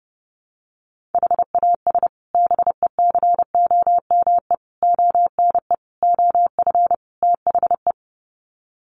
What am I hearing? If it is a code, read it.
Morse code, heard as a keyed tone, about 30 wpm, 700 Hz. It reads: HAS BECOME ONE OF THE
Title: Sets of 5 Words Encore 30wpm